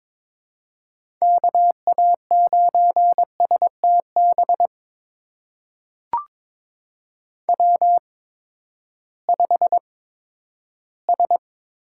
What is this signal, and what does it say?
Morse code, heard as a keyed tone, about 22 wpm, spaced as for 12 wpm, 700 Hz. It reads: KA9STB W5S
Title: USA Call Signs - 8x Character Spacing - 22wpm